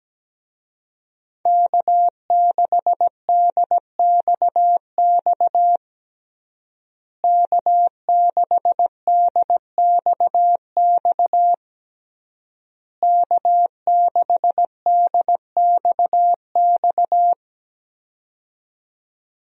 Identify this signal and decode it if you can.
Morse code, heard as a keyed tone, about 17 wpm, 700 Hz. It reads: K6DXX K6DXX K6DXX
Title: USA Call Signs - 5 Characters - Repeated 3x - 17wpm